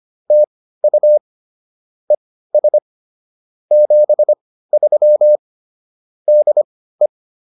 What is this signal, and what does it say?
Morse code, heard as a keyed tone, about 24 wpm, spaced as for 15 wpm, 600 Hz. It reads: TU ES 73 DE